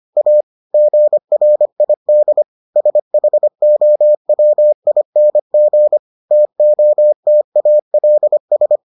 Morse code, heard as a keyed tone, about 25 wpm, 600 Hz. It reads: A GRID SHOWING TOTALS